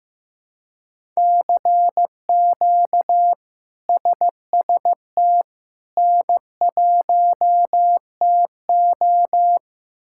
Morse code, heard as a keyed tone, about 15 wpm, 700 Hz. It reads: CQ SST N1TO
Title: K1USN SST QSO Practice - 15wpm